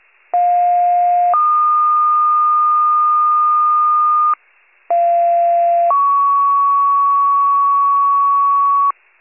Two-tone page, heard as a one-second tone, then a three-second tone, about 710 then 1220 Hz, about 690 then 1120 Hz. It is heard from 0.3 to 8.9 s.